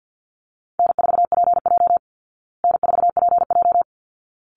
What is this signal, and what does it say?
Morse code, heard as a keyed tone, about 50 wpm, 700 Hz. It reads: N4PJ N4PJ
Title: Easy Call Signs Repeated 2x - 50wpm